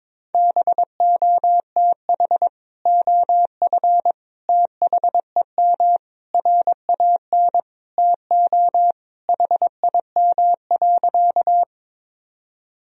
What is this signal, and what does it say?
Morse code, heard as a keyed tone, about 22 wpm, 700 Hz. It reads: BOTH OF THEM RAN TO HIM.
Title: Sentences from Top 1000 Words - Review Part 3 - 22wpm